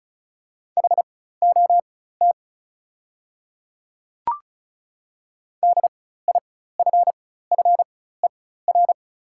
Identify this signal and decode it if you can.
Morse code, heard as a keyed tone, about 36 wpm, spaced as for 16 wpm, 700 Hz. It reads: HOT DIFFER
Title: 4x Character Spacing - Top 200 Words - 35wpm